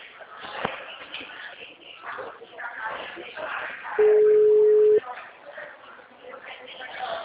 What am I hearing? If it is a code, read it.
Ringback tone (Europe, 425 Hz), with speech.